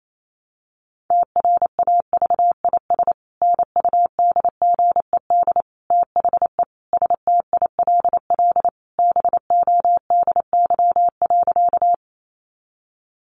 Morse code, heard as a keyed tone, about 28 wpm, 700 Hz. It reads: TRAVIS NUDGED THE STILL BODY.
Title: A Sound Of Thunder 28wpm